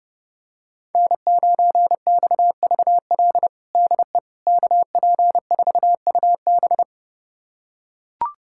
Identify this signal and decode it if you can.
Morse code, heard as a keyed tone, about 30 wpm, 700 Hz. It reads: N9XVL DE KP4UB
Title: USA Callsign DE Callsign 30wpm